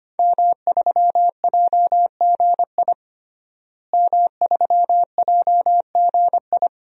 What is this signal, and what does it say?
Morse code, heard as a keyed tone, about 25 wpm, 700 Hz. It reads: M3JGI M3JGI
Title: DXCC-100 Call Signs - 5 Characters - Repeated 2x - 25wpm